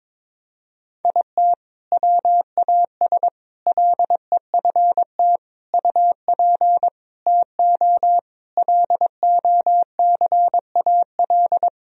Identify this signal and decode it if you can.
Morse code, heard as a keyed tone, about 21 wpm, 700 Hz. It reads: IT WAS LEFT UP TO LOCAL
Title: Sentences from Top 1000 Words - Review Part 4 - 22wpm